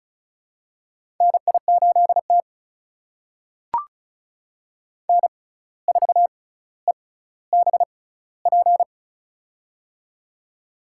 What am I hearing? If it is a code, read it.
Morse code, heard as a keyed tone, about 35 wpm, spaced as for 17 wpm, 700 Hz. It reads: NI8T N4EBP
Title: USA Call Signs - 6x Character Spacing - 35wpm